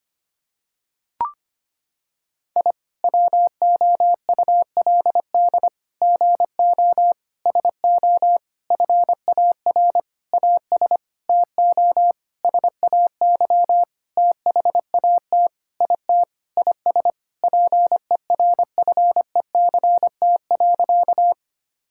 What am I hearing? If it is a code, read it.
Morse code, heard as a keyed tone, about 25 wpm, 700 Hz. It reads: I WOULD GO SO FAR AS TO SAY THAT IT IS PERFECT.
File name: practice / 25wpm / Sentences with 9-Word Phrases - 25wpm